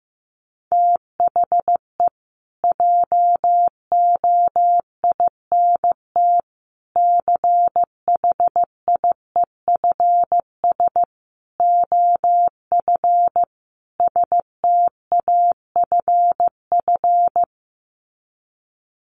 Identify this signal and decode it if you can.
Morse code, heard as a keyed tone, about 15 wpm, 700 Hz. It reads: THE JOINT CHIEFS OF STAFF
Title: Sets of 5 Words Encore2 15wpm